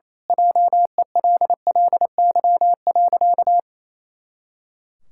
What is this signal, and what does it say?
Morse code, heard as a keyed tone, about 28 wpm, 700 Hz. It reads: JELLY.
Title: Pangram Challenge v2 - 28wpm